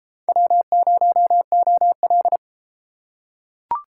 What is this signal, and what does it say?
Morse code, heard as a keyed tone, about 28 wpm, 700 Hz. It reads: W0OL